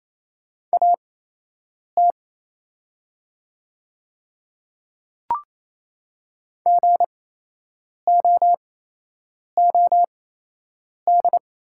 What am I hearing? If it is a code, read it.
Morse code, heard as a keyed tone, about 28 wpm, spaced as for 8 wpm, 700 Hz. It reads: AT GOOD
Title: 8x Character Spacing - Top 300 Words - 28wpm